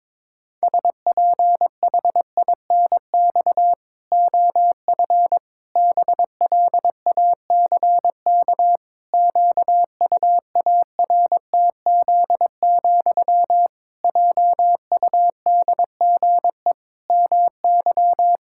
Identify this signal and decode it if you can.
Morse code, heard as a keyed tone, about 21 wpm, 700 Hz. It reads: SPHINX OF BLACK QUARTZ, JUDGE MY
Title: Pangram Challenge v2 - 22wpm